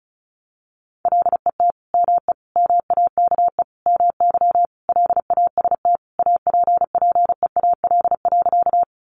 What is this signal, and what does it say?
Morse code, heard as a keyed tone, about 34 wpm, 700 Hz. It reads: LET ME MAKE MY LAST APPEAL.
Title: Sentences from Top 1300 Words 35wpm